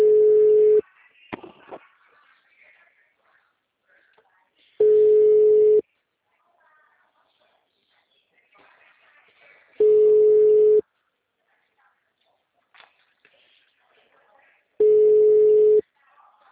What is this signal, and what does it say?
Ringback tone (Europe, 425 Hz), with speech.
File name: out- -2001- - - .22